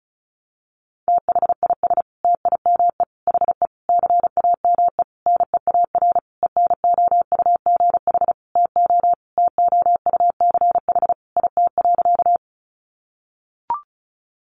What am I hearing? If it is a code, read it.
Morse code, heard as a keyed tone, about 34 wpm, 700 Hz. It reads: THIS TIME HE CAME NEAR ENOUGH TO TOUCH IT.
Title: Sentences from Top 1200 Words 35wpm